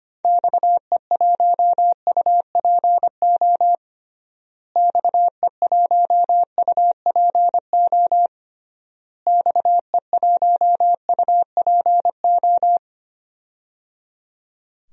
Morse code, heard as a keyed tone, about 25 wpm, 700 Hz. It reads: XE1UPO XE1UPO XE1UPO
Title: DXCC-100 Call Signs - 6 Characters - Repeated 3x - 25wpm